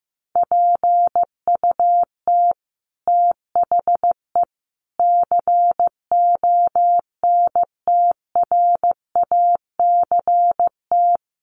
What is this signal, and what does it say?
Morse code, heard as a keyed tone, about 15 wpm, 700 Hz. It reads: PUT THE CONTRACT
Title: Sentences from Top 1400 Words 15wpm